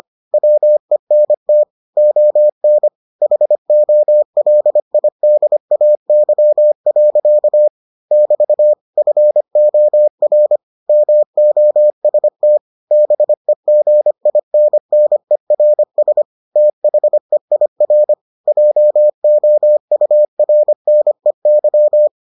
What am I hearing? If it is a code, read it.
Morse code, heard as a keyed tone, about 25 wpm, 600 Hz. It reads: WENT ON HOLIDAY. <BT> FOR MOST BEGINNERS THEIR JOURNEY